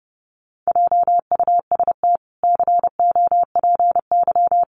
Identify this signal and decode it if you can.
Morse code, heard as a keyed tone, about 30 wpm, 700 Hz. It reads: JUST COPY
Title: Sentences from Top 1400 Words 30wpm